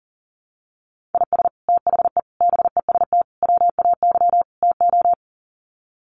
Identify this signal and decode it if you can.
Morse code, heard as a keyed tone, about 40 wpm, 700 Hz. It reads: IS THE BEST WAY TO
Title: Sets of 5 Words Encore 40wpm